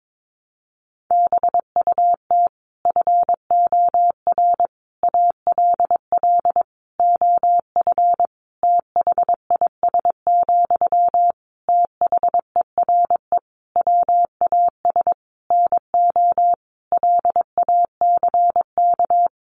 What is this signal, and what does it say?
Morse code, heard as a keyed tone, about 22 wpm, 700 Hz. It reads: BUT FOR ALL OF THIS, THERE WAS NO LACK
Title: Sentences from Top 1300 Words 22wpm